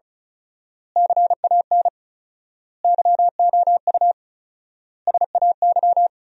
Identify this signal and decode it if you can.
Morse code, heard as a keyed tone, about 36 wpm, 700 Hz. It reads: CAN YOU SAY
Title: 4x Word-Spacing - Sentences from Top 200 Words - 35wpm